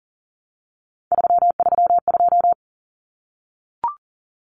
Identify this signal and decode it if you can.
Morse code, heard as a keyed tone, about 40 wpm, 700 Hz. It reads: 332
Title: Numbers - 3-Digit 40wpm